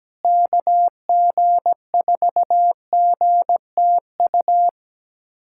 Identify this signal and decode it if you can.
Morse code, heard as a keyed tone, about 17 wpm, 700 Hz. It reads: KG4GTU
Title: USA Callsign DE Callsign - 6 Characters - 17wpm